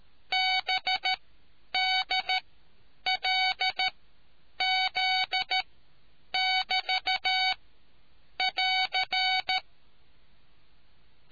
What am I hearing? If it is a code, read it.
Morse code, heard as a keyed tone, about 13 wpm, spaced as for 8 wpm, 750 Hz. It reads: BDLZ<BT><AR>